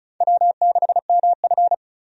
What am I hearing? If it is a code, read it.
Morse code, heard as a keyed tone, about 36 wpm, 700 Hz. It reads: W6MF
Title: USA Call Signs - 4 Characters - 35wpm